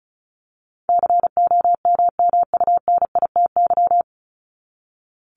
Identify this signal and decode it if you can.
Morse code, heard as a keyed tone, about 34 wpm, 700 Hz. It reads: COMMUNITY
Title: Top 501-750 Words 35wpm